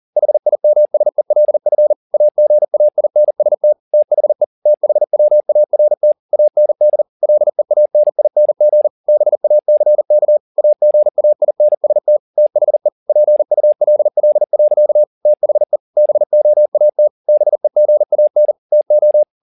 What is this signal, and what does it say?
Morse code, heard as a keyed tone, about 40 wpm, 600 Hz. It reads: HIMSELF AGAINST THE THWART AND LEANING BACK AGAINST THE PULL. THE BOAT BEGAN TO